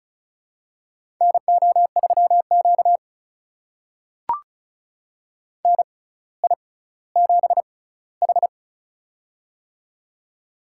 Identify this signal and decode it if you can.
Morse code, heard as a keyed tone, about 36 wpm, spaced as for 18 wpm, 700 Hz. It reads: NO3Q NI7H
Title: USA Call Signs - 4 Characters - 6x Character Spacing - 35wpm